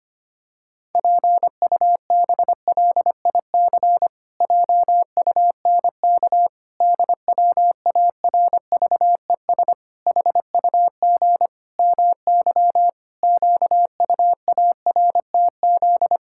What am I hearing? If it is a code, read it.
Morse code, heard as a keyed tone, about 25 wpm, 700 Hz. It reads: PUBLIC JUNK DWARVES HUG MY QUARTZ
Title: Pangram Challenge v2 - 25wpm